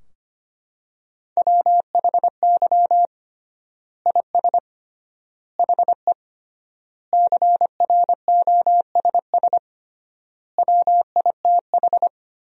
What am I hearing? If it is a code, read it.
Morse code, heard as a keyed tone, about 25 wpm, 700 Hz. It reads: WHY IS HE CROSS WITH
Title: 3x Word-Spacing - Sentences from Top 300 Words - 25wpm